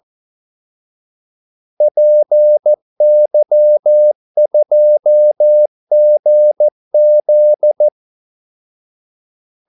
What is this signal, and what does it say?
Morse code, heard as a keyed tone, about 14 wpm, 600 Hz. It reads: PY2GZ
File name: Callsigns-14wpm